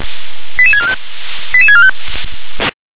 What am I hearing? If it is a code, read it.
Five-tone selective call (ZVEI), heard as quick runs of five tones, tones of about 70 ms. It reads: 88645 88645
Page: /357/SDR/ZVEI/